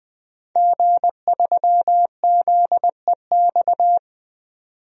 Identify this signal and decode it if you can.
Morse code, heard as a keyed tone, about 20 wpm, 700 Hz. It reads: G3ZEX
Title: DXCC-100 Call Signs - 5 Characters - 20wpm